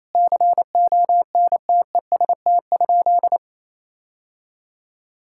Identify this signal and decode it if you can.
Morse code, heard as a keyed tone, about 28 wpm, 700 Hz. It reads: CONTEST?
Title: Sentences from Top 1600 Words 28wpm